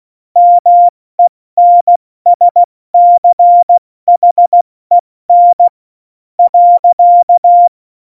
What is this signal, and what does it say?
Morse code, heard as a keyed tone, about 16 wpm, spaced as for 13 wpm, 700 Hz. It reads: MENSCHEN.
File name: Morsekurs / Witze / Witz 28. Januar 2020, 70 F80